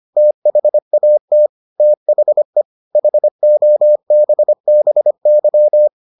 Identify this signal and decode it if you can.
Morse code, heard as a keyed tone, about 25 wpm, 600 Hz. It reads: THAT THE HOBBY